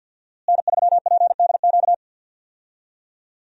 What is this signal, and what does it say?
Morse code, heard as a keyed tone, about 51 wpm, 700 Hz. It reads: N3PDQ
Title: USA Call Signs - 5 Characters - Repeated 2x - 50wpm